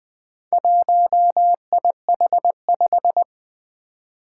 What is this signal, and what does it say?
Morse code, heard as a keyed tone, about 20 wpm, 700 Hz. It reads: 1IH5
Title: Four Challenging Letter-Number Combinations 20wpm